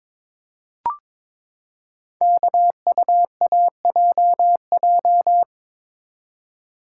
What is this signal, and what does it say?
Morse code, heard as a keyed tone, about 22 wpm, 700 Hz. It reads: KUAJJ
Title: Code Groups - 5 Letters - 22wpm